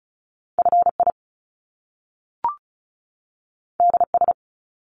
Morse code, heard as a keyed tone, about 36 wpm, 700 Hz. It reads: FI DS
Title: Top Two Letter Combinations in Words 35wpm